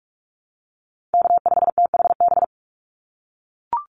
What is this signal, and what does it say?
Morse code, heard as a keyed tone, about 44 wpm, 700 Hz. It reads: K5THB
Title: USA Call Signs 45wpm